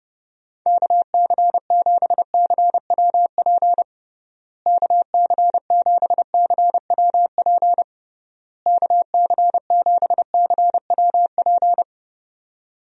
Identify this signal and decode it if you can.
Morse code, heard as a keyed tone, about 32 wpm, 700 Hz. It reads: KC7CWP KC7CWP KC7CWP
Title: USA Call Signs - 6 Characters - Repeated 3x - 30wpm